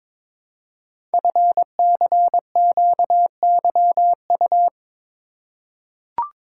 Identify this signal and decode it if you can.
Morse code, heard as a keyed tone, about 22 wpm, 700 Hz. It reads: FCQYU
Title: Code Groups - 5 Letters - 22wpm